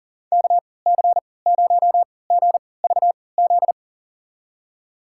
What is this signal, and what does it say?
Morse code, heard as a keyed tone, about 40 wpm, spaced as for 23 wpm, 700 Hz. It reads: KC0GVZ
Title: USA Call Signs - 6 Characters - 3x Character Spacing - 40wpm